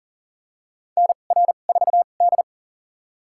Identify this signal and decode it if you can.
Morse code, heard as a keyed tone, about 40 wpm, spaced as for 29 wpm, 700 Hz. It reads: NR4D
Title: USA Call Signs - 2x Character Spacing - 40wpm